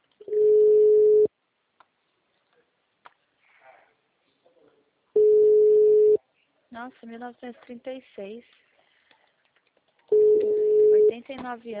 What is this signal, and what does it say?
Ringback tone (Europe, 425 Hz), with speech.